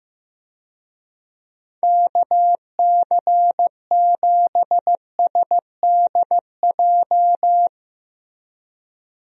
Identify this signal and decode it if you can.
Morse code, heard as a keyed tone, about 16 wpm, 700 Hz. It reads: KC7SDJ
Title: USA Call Signs - 6 Characters - Repeated 3x - 15wpm